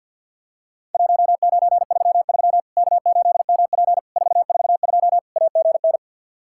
Morse code, heard as a keyed tone, about 50 wpm, 700 Hz. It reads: 1933 K8MP 442 AGN
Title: Worked All Europe DX Contest - QTC Exchange with Repeats - 50wpm